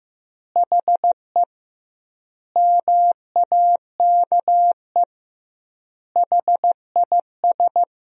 Morse code, heard as a keyed tone, about 15 wpm, 700 Hz. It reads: HE MAKE HIS
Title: 2x Word-Spacing - Sentences from Top 400 Words - 15wpm